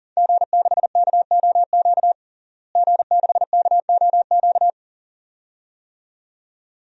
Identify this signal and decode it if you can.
Morse code, heard as a keyed tone, about 40 wpm, 700 Hz. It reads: G6KOQ G6KOQ
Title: DXCC-100 Call Signs - 5 Characters - Repeated 2x - 40wpm